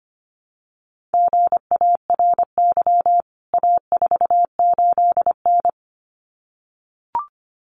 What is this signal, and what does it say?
Morse code, heard as a keyed tone, about 25 wpm, 700 Hz. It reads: GARY A48N
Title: CWT Exchange 25wpm